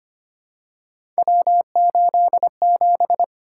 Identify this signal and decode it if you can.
Morse code, heard as a keyed tone, about 25 wpm, 700 Hz. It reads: W87
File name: Three Letter-Number Combinations - 25wpm